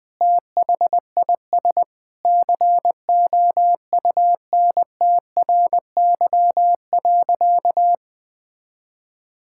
Morse code, heard as a keyed tone, about 20 wpm, 700 Hz. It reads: THIS COUNTRY.
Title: Sentences from Top 1000 Words - Review Part 5 - 20wpm